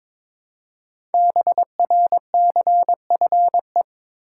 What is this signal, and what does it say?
Morse code, heard as a keyed tone, about 23 wpm, 700 Hz. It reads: BRCFE
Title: Code Groups - 5 Letters - 22wpm